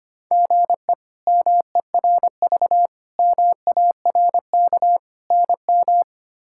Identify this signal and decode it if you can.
Morse code, heard as a keyed tone, about 25 wpm, 700 Hz. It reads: GE MERV MARK NM
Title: K1USN SST QSO Practice - 25wpm